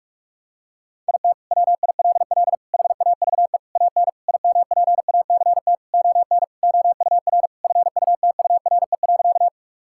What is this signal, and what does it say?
Morse code, heard as a keyed tone, about 41 wpm, 700 Hz. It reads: IT WILL HAVE AN IMPACT ON OUR FUTURE.
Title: Sentences from Top 1000 Words - Review Part 6 - 45wpm